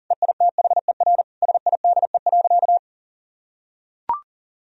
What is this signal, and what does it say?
Morse code, heard as a keyed tone, about 40 wpm, 700 Hz. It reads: EITHER SIDE.
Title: Sentences from Top 1000 Words - Review Part 4 - 40wpm